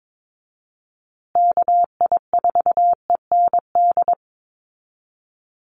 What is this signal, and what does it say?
Morse code, heard as a keyed tone, about 22 wpm, 700 Hz. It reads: KI4END
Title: USA Call Signs Repeated 3x 22wpm